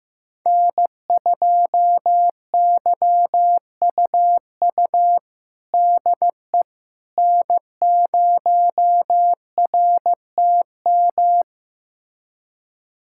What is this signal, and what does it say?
Morse code, heard as a keyed tone, about 15 wpm, 700 Hz. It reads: N2YUU DE N0RTM
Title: USA Callsign DE Callsign - 5 Characters - 15wpm